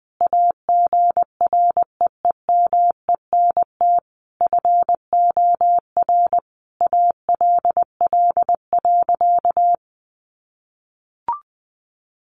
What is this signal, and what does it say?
Morse code, heard as a keyed tone, about 20 wpm, 700 Hz. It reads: AGREEMENT FOR ALL.
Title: Sentences from Top 1500 Words 20wpm